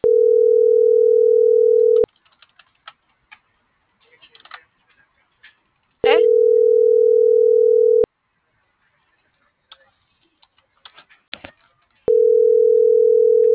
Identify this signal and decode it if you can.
Ringback tone (North America), 440 + 480 Hz.